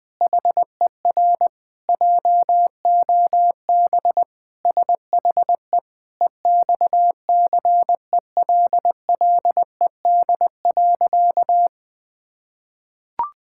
Morse code, heard as a keyed tone, about 21 wpm, 700 Hz. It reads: HER JOB SHE EXCELLED.
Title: Sentences with 6-Word Phrases - 20wpm